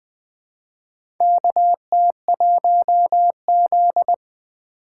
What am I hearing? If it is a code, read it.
Morse code, heard as a keyed tone, about 20 wpm, 700 Hz. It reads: KT1Z
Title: USA Call Signs - 4 Characters - 4x Character Spacing - 20wpm